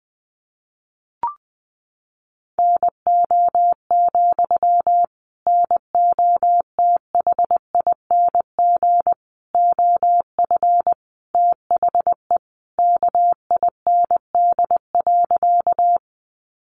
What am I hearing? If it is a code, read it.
Morse code, heard as a keyed tone, about 20 wpm, 700 Hz. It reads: NO, NOTHING OF THE KIND.
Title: Sentences from Top 500 Words 20wpm